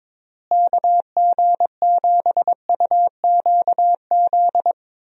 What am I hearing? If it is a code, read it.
Morse code, heard as a keyed tone, about 22 wpm, 700 Hz. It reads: KG7UQZ
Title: USA Call Signs - 6 Characters - Repeated 2x - 22wpm